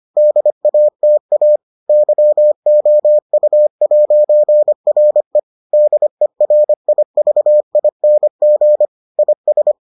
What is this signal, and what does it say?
Morse code, heard as a keyed tone, about 25 wpm, 600 Hz. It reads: DATA YOU'RE DERIVING IS